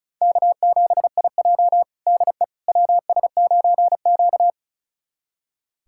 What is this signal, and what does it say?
Morse code, heard as a keyed tone, about 36 wpm, 700 Hz. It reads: K7IJ DE WS9Q
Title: USA Callsign DE Callsign - 4 Characters - 35wpm